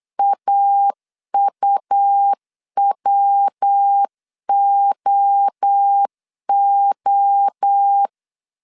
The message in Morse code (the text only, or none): AUWOO